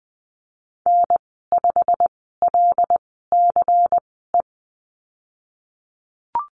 Morse code, heard as a keyed tone, about 20 wpm, spaced as for 14 wpm, 700 Hz. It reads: N5LCE
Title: 122 Lesson 20g - Number 3 - Call Signs 20wpm